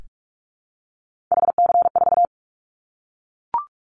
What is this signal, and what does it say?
Morse code, heard as a keyed tone, about 44 wpm, 700 Hz. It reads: HC4
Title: Callsign Prefixes for Top 100 DXCC - 45wpm